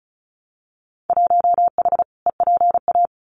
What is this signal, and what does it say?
Morse code, heard as a keyed tone, about 36 wpm, 700 Hz. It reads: 1H EPA
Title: Winter Field Day Exchanges Encore 35wpm